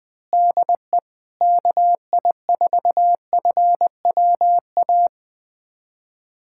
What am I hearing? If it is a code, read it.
Morse code, heard as a keyed tone, about 20 wpm, 700 Hz. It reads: DE KI4FWA
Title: USA Callsign DE Callsign - 6 Characters - 20wpm